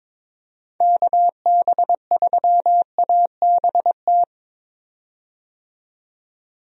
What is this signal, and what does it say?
Morse code, heard as a keyed tone, about 22 wpm, 700 Hz. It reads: KB3ABT
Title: USA Call Signs - 6 Characters - 22wpm